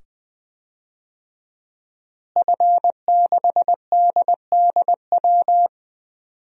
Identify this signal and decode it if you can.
Morse code, heard as a keyed tone, about 20 wpm, 700 Hz. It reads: F6DDW
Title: DXCC-100 Call Signs - 5 Characters - Repeated 2x - 20wpm